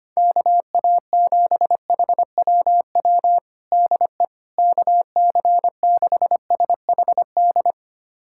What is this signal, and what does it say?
Morse code, heard as a keyed tone, about 25 wpm, 700 Hz. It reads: KA7HWW DE KC6SHD
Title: USA Callsign DE Callsign 25wpm